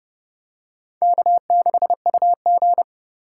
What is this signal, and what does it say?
Morse code, heard as a keyed tone, about 31 wpm, 700 Hz. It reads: K6UG